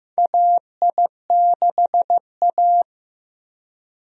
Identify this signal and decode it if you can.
Morse code, heard as a keyed tone, about 16 wpm, 700 Hz. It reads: AI6A
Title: USA Call Signs - 4 Characters - Repeated 2x - 15wpm